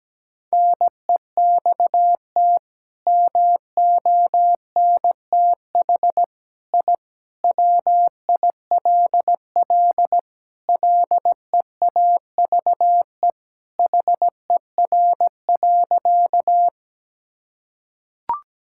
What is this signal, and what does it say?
Morse code, heard as a keyed tone, about 17 wpm, 700 Hz. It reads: NEXT MONTH I WILL LEAVE HER.
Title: Sentences from Top 1000 Words - Review Part 5 - 17wpm